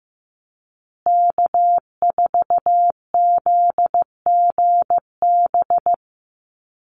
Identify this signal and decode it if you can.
Morse code, heard as a keyed tone, about 15 wpm, 700 Hz. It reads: K4ZGB
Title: Call Signs Repeated 2x - 15wpm